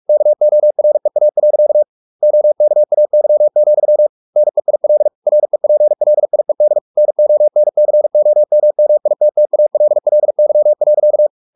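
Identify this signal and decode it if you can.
Morse code, heard as a keyed tone, about 44 wpm, 600 Hz. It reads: KOREA. OKAY, NEIL REPLIED NONCOMMITTALLY.
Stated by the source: Morse practice at 45 wpm, the novel Never by Ken Follett in English, from CW med Gnister.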